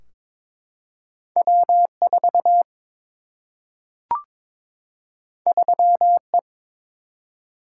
Morse code, heard as a keyed tone, about 22 wpm, 700 Hz. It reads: W 4 3 E
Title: Single-Letter Single-Number 22wpm